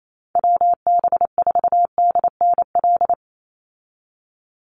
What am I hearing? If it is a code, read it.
Morse code, heard as a keyed tone, about 28 wpm, 700 Hz. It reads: WB4DNL
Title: Call Signs - 28wpm